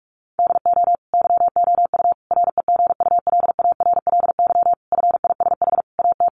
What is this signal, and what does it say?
Morse code, heard as a keyed tone, about 44 wpm, 700 Hz. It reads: DO YOU REGULARLY FISH AT
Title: Sentences from Top 1400 Words 45wpm